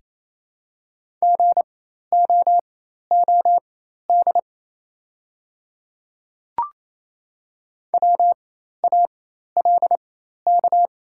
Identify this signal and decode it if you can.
Morse code, heard as a keyed tone, about 28 wpm, spaced as for 13 wpm, 700 Hz. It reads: GOOD WALK
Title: 4x Character Spacing - Top 300 Words - 28wpm